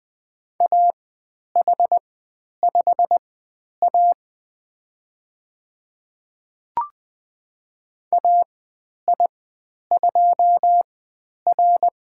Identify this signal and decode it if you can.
Morse code, heard as a keyed tone, about 20 wpm, spaced as for 10 wpm, 700 Hz. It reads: AH5A AI2R
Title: 095 Lesson 16g - Letter c - Call Signs 20wpm